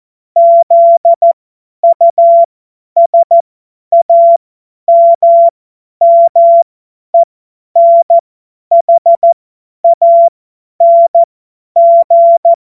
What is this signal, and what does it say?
Morse code, heard as a keyed tone, about 14 wpm, spaced as for 10 wpm, 680 Hz. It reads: ZUSAMMENHANG
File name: Morsekurs / Witze / Witz 14. Juni 2022 50 F70